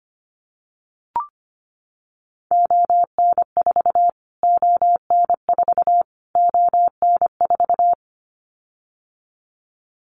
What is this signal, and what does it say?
Morse code, heard as a keyed tone, about 25 wpm, 700 Hz. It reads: ON4 ON4 ON4
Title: Callsign Prefixes for Top 25 DXCC - Repeated 3x - 25wpm